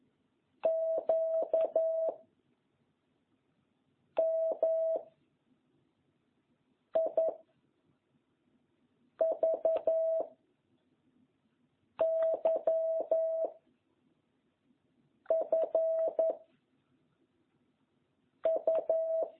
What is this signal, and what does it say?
Morse code, heard as a keyed tone, about 11 wpm, 660 Hz. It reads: Q M I V Y F U